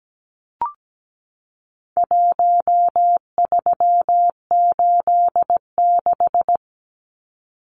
Morse code, heard as a keyed tone, about 17 wpm, 700 Hz. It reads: 1386
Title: Numbers - Contest Style - 4-Digit 17wpm